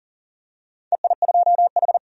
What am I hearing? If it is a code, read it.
Morse code, heard as a keyed tone, about 40 wpm, 700 Hz. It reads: EI2H
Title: DXCC-100 Call Signs - 4 Characters - 40wpm